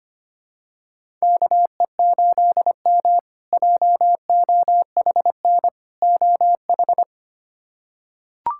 Morse code, heard as a keyed tone, about 25 wpm, 700 Hz. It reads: KE8M JOHN OH
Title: K1USN SST QSO Practice - 25wpm